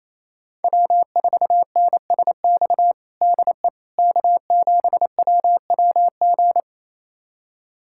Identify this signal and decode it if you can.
Morse code, heard as a keyed tone, about 28 wpm, 700 Hz. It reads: W4NSX DE K7WWG
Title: USA Callsign DE Callsign 28wpm